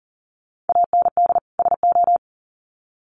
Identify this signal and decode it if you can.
Morse code, heard as a keyed tone, about 40 wpm, 700 Hz. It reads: AND SO
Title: Sets of 2 Words 40wpm